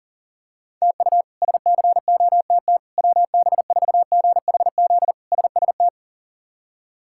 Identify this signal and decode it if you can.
Morse code, heard as a keyed tone, about 40 wpm, 700 Hz. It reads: TU SCOTT WB4GHZ SST
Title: K1USN SST QSO Practice - 40wpm